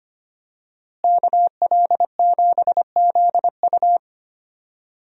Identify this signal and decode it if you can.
Morse code, heard as a keyed tone, about 25 wpm, 700 Hz. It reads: KL7ZU
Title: USA Call Signs - 5 Characters - Repeated 2x - 25wpm